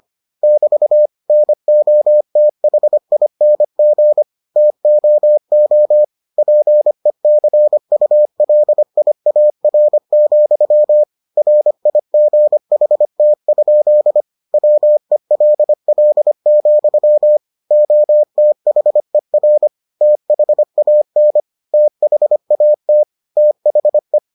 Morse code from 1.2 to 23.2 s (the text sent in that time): NOTHING TOO PECULIAR, RIGHT? WELL, OTHER THAN THAT